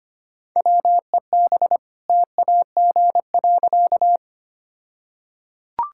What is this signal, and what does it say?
Morse code, heard as a keyed tone, about 25 wpm, 700 Hz. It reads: WEB TAG.
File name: practice / 25wpm / All-Age Appropriate Jokes - 25wpm